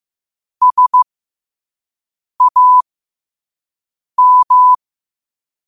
Morse code, heard as a keyed tone, about 15 wpm, spaced as for 5 wpm, 1000 Hz. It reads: SAM
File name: qotdfiles / 5wpm